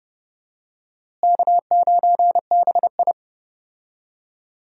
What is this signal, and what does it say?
Morse code, heard as a keyed tone, about 32 wpm, 700 Hz. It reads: K9BI
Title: USA Call Signs - 4 Characters - Repeated 2x - 30wpm